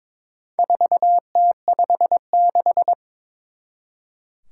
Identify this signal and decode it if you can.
Morse code, heard as a keyed tone, about 21 wpm, 700 Hz. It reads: 4T56
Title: Four Challenging Letter-Number Combinations 22wpm